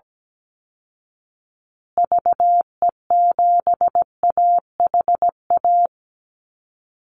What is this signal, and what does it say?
Morse code, heard as a keyed tone, about 17 wpm, 700 Hz. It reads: VE7AHA
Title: Call Signs Repeated 2x - 17wpm